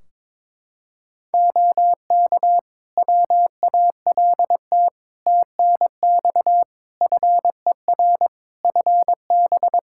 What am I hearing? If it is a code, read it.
Morse code, heard as a keyed tone, about 22 wpm, 700 Hz. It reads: OK WALT TNX FER FB
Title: Practice QSO - Exchange 4 - Ending - 22wpm